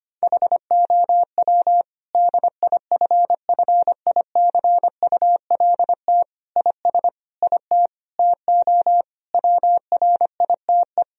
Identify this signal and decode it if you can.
Morse code, heard as a keyed tone, about 25 wpm, 700 Hz. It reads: HOW DIFFICULT IS IT TO WRITE
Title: Sentences from Top 1000 Words - Review Part 3 - 25wpm